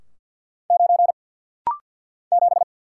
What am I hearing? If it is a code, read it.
Morse code, heard as a keyed tone, about 48 wpm, 700 Hz. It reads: 9 7
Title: Single Letter-Number Alt - Rapid-Fire 50wpm